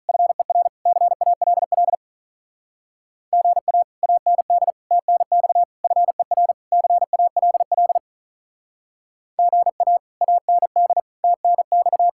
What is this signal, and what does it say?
Morse code, heard as a keyed tone, about 43 wpm, 700 Hz. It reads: FER CALL GA AND TNX FER CALL GA AND TNX
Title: Speed-Racing Common QSO Phrases - 35wpm